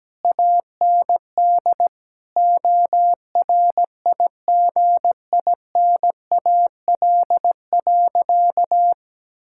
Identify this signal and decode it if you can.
Morse code, heard as a keyed tone, about 17 wpm, 700 Hz. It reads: AND ORIGINAL.
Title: Sentences from Top 1000 Words - Review Part 6 - 17wpm